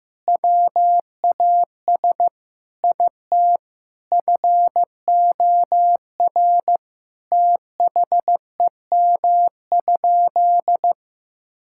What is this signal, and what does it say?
Morse code, heard as a keyed tone, about 15 wpm, 700 Hz. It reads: WAS IT FOR THEM?
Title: Sentences from Top 1000 Words - Review Part 5 - 15wpm